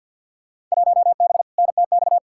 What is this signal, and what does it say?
Morse code, heard as a keyed tone, about 53 wpm, 700 Hz. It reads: 1B NTX
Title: Field Day Exchanges - 50wpm